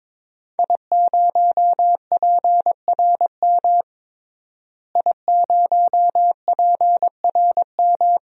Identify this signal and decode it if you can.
Morse code, heard as a keyed tone, about 21 wpm, 700 Hz. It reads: I0PRM I0PRM
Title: DXCC-100 Call Signs - 5 Characters - Repeated 3x - 22wpm